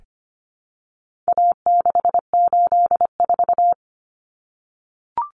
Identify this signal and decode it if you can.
Morse code, heard as a keyed tone, about 25 wpm, 700 Hz. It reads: A684
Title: Common Cut Numbers - 4-Digit 25wpm